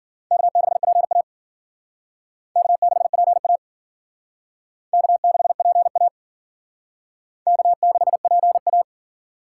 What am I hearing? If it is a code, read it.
Morse code, heard as a keyed tone, about 52 wpm, 700 Hz. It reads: K6PA K6PA K6PA K6PA